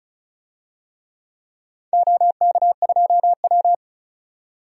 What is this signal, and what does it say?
Morse code, heard as a keyed tone, about 34 wpm, 700 Hz. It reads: OK2W
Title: DXCC-100 Call Signs - 4 Characters - 35wpm